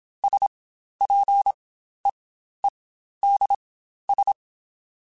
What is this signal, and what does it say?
Morse code, heard as a keyed tone, about 27 wpm, spaced as for 12 wpm, 800 Hz. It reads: SPEEDS
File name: Spaced Characters / Spaced Characters-mp3s / Weather4 14 sp @25